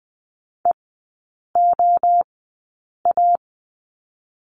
Morse code, heard as a keyed tone, about 19 wpm, 700 Hz. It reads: E O A
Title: 023 Lesson 5e - Letter o - Three Characters 20wpm